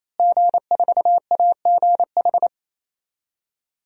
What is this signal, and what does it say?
Morse code, heard as a keyed tone, about 28 wpm, 700 Hz. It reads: G4AGH
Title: DXCC-100 Call Signs - 5 Characters - Repeated 3x - 28wpm